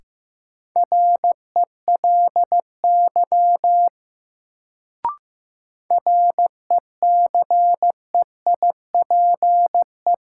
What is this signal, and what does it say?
Morse code, heard as a keyed tone, about 15 wpm, 700 Hz. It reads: RELY RECEIPE
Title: Sentences from Top 1200 Words - New Word Drill Alt - 15wpm